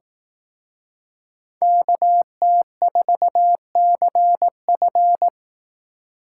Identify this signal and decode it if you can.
Morse code, heard as a keyed tone, about 18 wpm, 700 Hz. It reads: KT4CF
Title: Speed Racing - USA Call Signs 15wpm